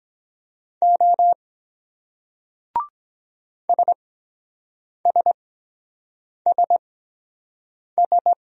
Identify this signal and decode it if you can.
Morse code, heard as a keyed tone, about 23 wpm, spaced as for 7 wpm, 700 Hz. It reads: O SSSS